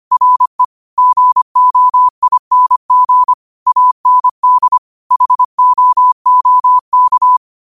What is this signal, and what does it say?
Morse code, heard as a keyed tone, about 25 wpm, 1000 Hz. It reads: RE GOING AND HOOK
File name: qotdfiles / 25wpm